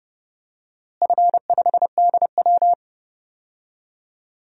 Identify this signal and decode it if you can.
Morse code, heard as a keyed tone, about 30 wpm, 700 Hz. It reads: F5DW
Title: DXCC-100 Call Signs - 4 Characters - Repeated 3x - 30wpm